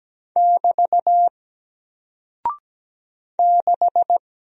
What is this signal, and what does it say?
Morse code, heard as a keyed tone, about 17 wpm, 700 Hz. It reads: <BT> 6
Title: Single Character (Letter:Number:Punctuation:Prosign) Alt 17wpm